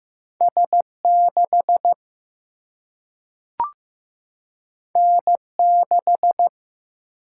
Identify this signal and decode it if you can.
Morse code, heard as a keyed tone, about 15 wpm, 700 Hz. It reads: S 6 N 6
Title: Single-Letter Single-Number 15wpm